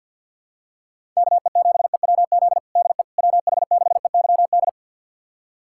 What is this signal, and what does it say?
Morse code, heard as a keyed tone, about 51 wpm, 700 Hz. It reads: KE7EWZ DE WH6EYD
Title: USA Callsign DE Callsign 50wpm